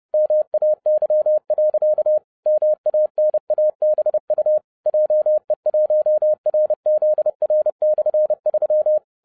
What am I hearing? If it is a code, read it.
Morse code, heard as a keyed tone, about 30 wpm, 610 Hz. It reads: MAY. MANABU JE1RZR/3
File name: 30wpm